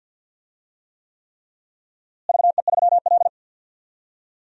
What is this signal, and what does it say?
Morse code, heard as a keyed tone, about 53 wpm, 700 Hz. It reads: VE3L
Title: DXCC-100 Call Signs - 4 Characters - Repeated 3x - 50wpm